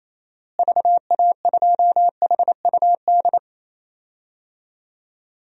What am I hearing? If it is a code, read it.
Morse code, heard as a keyed tone, about 28 wpm, 700 Hz. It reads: VA2HUD
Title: DXCC-100 Call Signs - 6 Characters - Repeated 2x - 28wpm